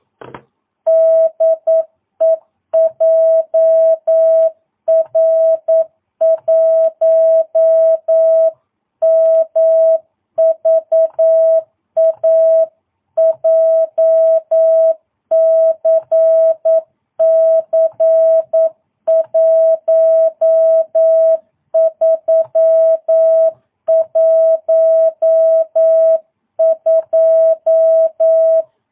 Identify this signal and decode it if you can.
Morse code, heard as a keyed tone, about 9 wpm, 650 Hz. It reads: DE JR1MVA JCC 1312
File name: jh8cbh / 2022cw